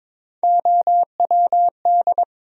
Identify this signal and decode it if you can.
Morse code, heard as a keyed tone, about 22 wpm, 700 Hz. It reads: OWD
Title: USA Call Sign Postfixes - 3 Letters 22wpm